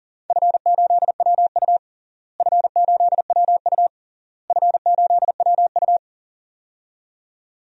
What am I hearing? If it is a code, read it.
Morse code, heard as a keyed tone, about 37 wpm, 700 Hz. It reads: F8WU F8WU F8WU
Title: DXCC-100 Call Signs - 4 Characters - Repeated 3x - 40wpm